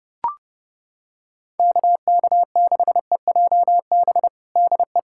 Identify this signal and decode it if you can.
Morse code, heard as a keyed tone, about 30 wpm, 700 Hz. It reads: KK6EJB DE
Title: USA Callsign DE Callsign - 6 Characters - 30wpm